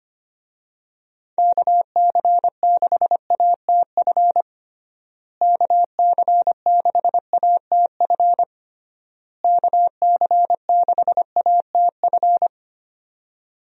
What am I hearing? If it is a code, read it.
Morse code, heard as a keyed tone, about 25 wpm, 700 Hz. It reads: KC6ATF KC6ATF KC6ATF
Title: USA Call Signs - 6 Characters - Repeated 3x - 25wpm